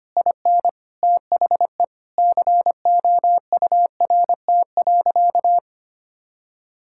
Morse code, heard as a keyed tone, about 25 wpm, 700 Hz. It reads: IN THE COURT.
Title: Sentences from Top 1600 Words 25wpm